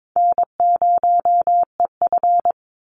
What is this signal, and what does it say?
Morse code, heard as a keyed tone, about 22 wpm, 700 Hz. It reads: N0EF
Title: Easy Call Signs Repeated 2x - 22wpm